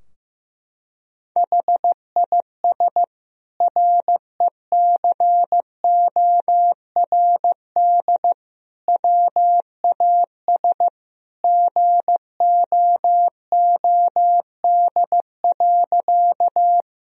Morse code, heard as a keyed tone, about 15 wpm, 700 Hz. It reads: HIS RECORD WAS GOOD.
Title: Sentences from Top 1000 Words - Review Part 3 - 15wpm